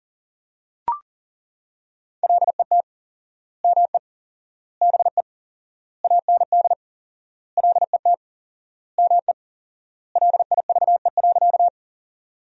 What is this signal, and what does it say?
Morse code, heard as a keyed tone, about 40 wpm, 700 Hz. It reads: LET ME BE AND LET ME LIVE.
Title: 4x Word-Spacing - Sentences from Top 300 Words - 40wpm